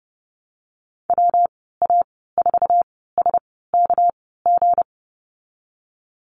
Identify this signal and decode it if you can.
Morse code, heard as a keyed tone, about 30 wpm, spaced as for 17 wpm, 700 Hz. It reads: WA4SKG
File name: Lesson 37g - Letter x - Call Signs 30wpm